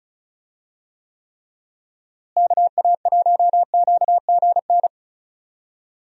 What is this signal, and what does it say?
Morse code, heard as a keyed tone, about 36 wpm, 700 Hz. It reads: KA1QGN
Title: USA Call Signs - 6 Characters - Repeated 3x - 35wpm